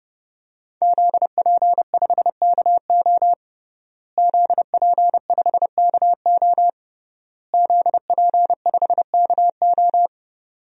Morse code, heard as a keyed tone, about 30 wpm, 700 Hz. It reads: ZP5KO ZP5KO ZP5KO
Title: DXCC-100 Call Signs - 5 Characters - Repeated 3x - 30wpm